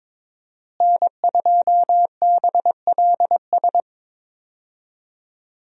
Morse code, heard as a keyed tone, about 22 wpm, 700 Hz. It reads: N2BLS
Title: USA Call Signs - 5 Characters - Repeated 3x - 22wpm